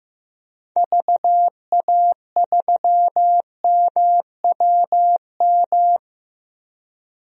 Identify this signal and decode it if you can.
Morse code, heard as a keyed tone, about 15 wpm, 700 Hz. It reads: VA3MWM
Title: DXCC-100 Call Signs - 6 Characters - 15wpm